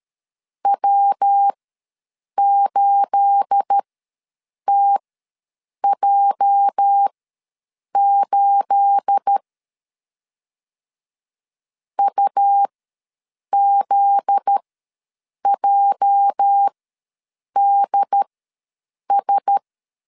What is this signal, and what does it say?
Morse code, heard as a keyed tone, about 13 wpm, spaced as for 7 wpm, 800 Hz. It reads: W8TJ8 UZJDS
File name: LEZIONI / 9 28 ws 12wpm 3354